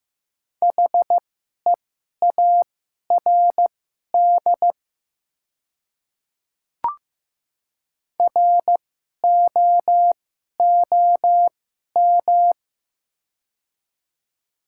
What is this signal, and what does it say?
Morse code, heard as a keyed tone, about 15 wpm, spaced as for 11 wpm, 700 Hz. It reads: HEARD ROOM
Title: 2x Character Spacing - Top 500 Words - 15wpm